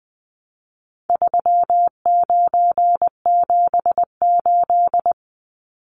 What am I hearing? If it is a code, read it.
Morse code, heard as a keyed tone, about 20 wpm, 700 Hz. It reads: 3978
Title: Numbers - 4-Digit 20wpm